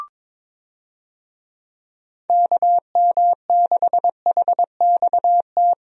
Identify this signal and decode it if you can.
Morse code, heard as a keyed tone, about 22 wpm, 700 Hz. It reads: KM6HXT
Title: USA Call Signs - 6 Characters - Repeated 2x - 22wpm